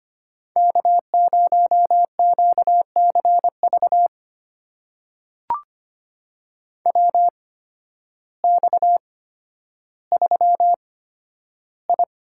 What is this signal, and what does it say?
Morse code, heard as a keyed tone, about 25 wpm, spaced as for 12 wpm, 700 Hz. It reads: K0QCV WX3I
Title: USA Call Signs - 8x Character Spacing - 25wpm